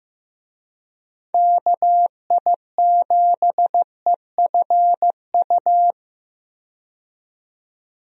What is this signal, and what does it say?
Morse code, heard as a keyed tone, about 15 wpm, 700 Hz. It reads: KI7EFU
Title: USA Call Signs - 6 Characters - 15wpm